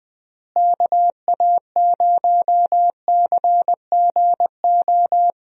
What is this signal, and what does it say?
Morse code, heard as a keyed tone, about 20 wpm, 700 Hz. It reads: KA0CGO
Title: USA Call Signs - 6 Characters - Repeated 3x - 20wpm